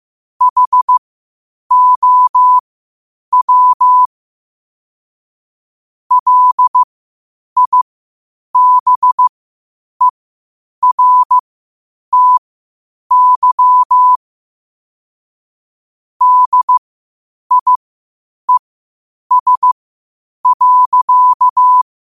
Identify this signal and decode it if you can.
Morse code, heard as a keyed tone, about 15 wpm, spaced as for 8 wpm, 1000 Hz. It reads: HOW LIBERTY DIES.
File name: qotdfiles / 8wpm